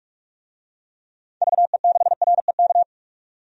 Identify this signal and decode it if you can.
Morse code, heard as a keyed tone, about 42 wpm, 700 Hz. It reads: VE6REK
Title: DXCC-100 Call Signs - 6 Characters - 45wpm